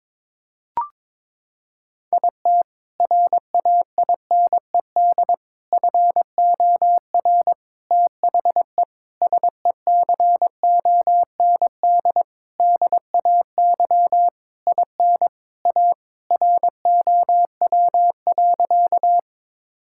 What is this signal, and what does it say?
Morse code, heard as a keyed tone, about 22 wpm, 700 Hz. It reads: IT RAINED FOR THE SECOND DAY IN A ROW.
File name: Sentences with 7-Word Phrases - 22wpm